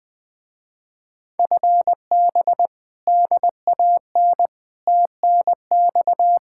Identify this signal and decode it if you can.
Morse code, heard as a keyed tone, about 20 wpm, 700 Hz. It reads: FB DAN TNX
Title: Speed-Racing Common QSO Phrases - 17wpm